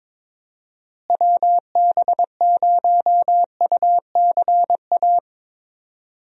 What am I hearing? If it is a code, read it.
Morse code, heard as a keyed tone, about 22 wpm, 700 Hz. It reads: WB0UCA
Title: USA Call Signs - 6 Characters - Repeated 2x - 22wpm